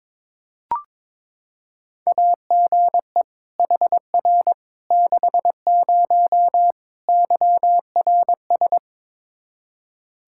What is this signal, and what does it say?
Morse code, heard as a keyed tone, about 22 wpm, 700 Hz. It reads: AGE HR 60 YRS
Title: Practice QSO - Exchange 3 - More Info - 22wpm